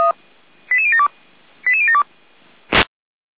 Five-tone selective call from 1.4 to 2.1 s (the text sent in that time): 88072